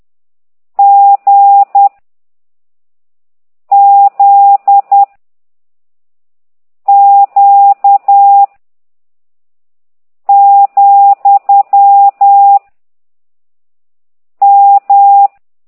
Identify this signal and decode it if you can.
Morse code, heard as a keyed tone, about 10 wpm, 800 Hz. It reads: G Z Q , M
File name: ct2idw / exerc15